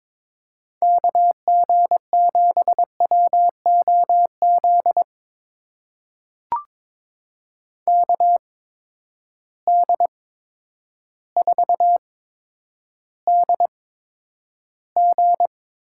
Morse code, heard as a keyed tone, about 21 wpm, spaced as for 10 wpm, 700 Hz. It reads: KG7WOZ KD4DG
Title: USA Call Signs - 6 Characters - 8x Character Spacing - 22wpm